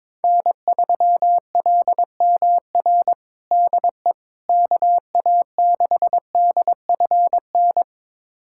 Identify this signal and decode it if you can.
Morse code, heard as a keyed tone, about 22 wpm, 700 Hz. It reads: N3LMR DE KA6DFN
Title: USA Callsign DE Callsign 22wpm